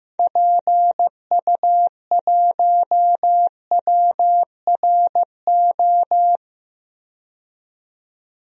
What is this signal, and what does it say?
Morse code, heard as a keyed tone, about 15 wpm, 700 Hz. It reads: PU1WRO
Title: DXCC-100 Call Signs - 6 Characters - 15wpm